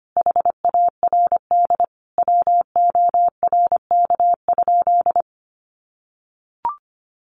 Morse code, heard as a keyed tone, about 25 wpm, 700 Hz. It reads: HARD WORK?
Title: Sentences from Top 1400 Words 25wpm